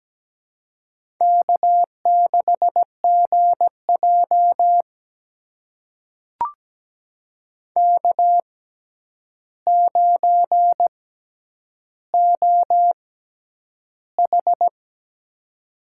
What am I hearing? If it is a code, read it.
Morse code, heard as a keyed tone, about 17 wpm, spaced as for 9 wpm, 700 Hz. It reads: K6GJ K9OH
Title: USA Call Signs - 4 Characters - 6x Character Spacing - 17wpm